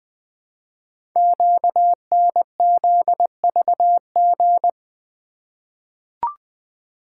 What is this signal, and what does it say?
Morse code, heard as a keyed tone, about 20 wpm, 700 Hz. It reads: QNZVG
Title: Code Groups - 5 Letters - 20wpm